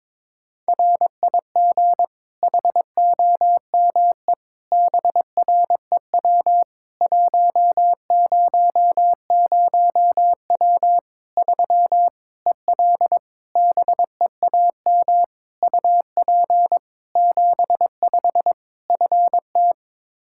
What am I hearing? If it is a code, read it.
Morse code, heard as a keyed tone, about 22 wpm, 700 Hz. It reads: RIG HOME BREW 100W 3 EL BEAM UP 75 FT
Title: K1USN SST QSO Practice 22wpm